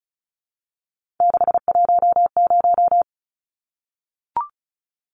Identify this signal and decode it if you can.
Morse code, heard as a keyed tone, about 34 wpm, 700 Hz. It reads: 610
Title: Numbers - 3-Digit 35wpm